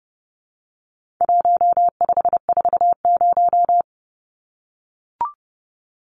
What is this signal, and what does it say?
Morse code, heard as a keyed tone, about 30 wpm, 700 Hz. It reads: 1540
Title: Numbers - 4-Digit 30wpm